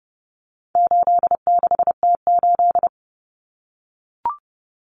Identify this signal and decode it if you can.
Morse code, heard as a keyed tone, about 30 wpm, 700 Hz. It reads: 86T8
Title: Common Cut Numbers - 4-Digit 30wpm